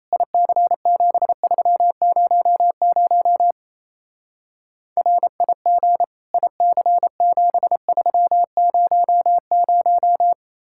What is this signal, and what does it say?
Morse code, heard as a keyed tone, about 36 wpm, 700 Hz. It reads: IC7300 RIG IC7300